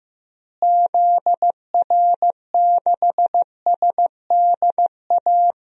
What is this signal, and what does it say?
Morse code, heard as a keyed tone, about 15 wpm, 700 Hz. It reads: ZR6SDA
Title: DXCC-100 Call Signs - 6 Characters - Repeated 3x - 15wpm